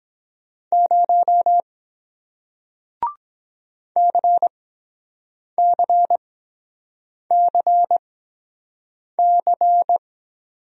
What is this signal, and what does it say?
Morse code, heard as a keyed tone, about 22 wpm, spaced as for 7 wpm, 700 Hz. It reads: 0 CCCC